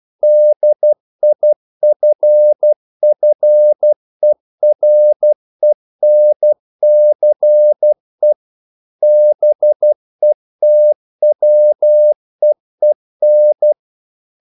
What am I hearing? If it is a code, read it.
Morse code, heard as a keyed tone, about 12 wpm, 600 Hz. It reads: DIFFERENCE BETWEEN
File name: CW / Practice / EnglishQuotations0112